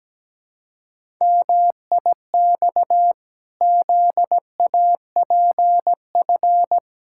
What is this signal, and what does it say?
Morse code, heard as a keyed tone, about 17 wpm, 700 Hz. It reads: MIX ZAPF
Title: Pangram Challenge v2 - 17wpm